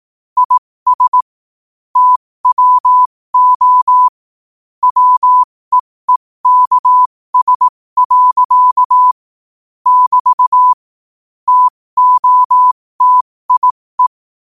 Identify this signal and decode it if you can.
Morse code, heard as a keyed tone, about 18 wpm, spaced as for 15 wpm, 1000 Hz. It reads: IS TWO WEEKS. <BT> TOTIE
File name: qotdfiles / 15wpm